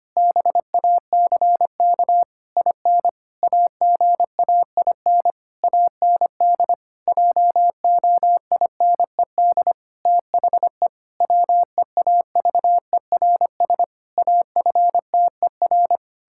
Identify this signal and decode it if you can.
Morse code, heard as a keyed tone, about 25 wpm, 700 Hz. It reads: BACK IN AGAIN AND JOINED THE WEAVERS AFTER
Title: Pangram Challenge v2 - 25wpm